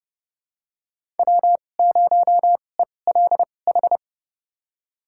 Morse code, heard as a keyed tone, about 30 wpm, spaced as for 22 wpm, 700 Hz. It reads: W0ELH
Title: USA Call Signs - 5 Characters - 2x Character Spacing - 30wpm